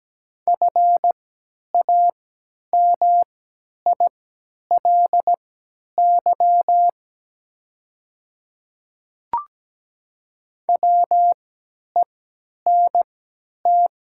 Morse code, heard as a keyed tone, about 17 wpm, spaced as for 10 wpm, 700 Hz. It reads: FAMILY WENT
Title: 3x Character Spacing - Top 500 Words - 17wpm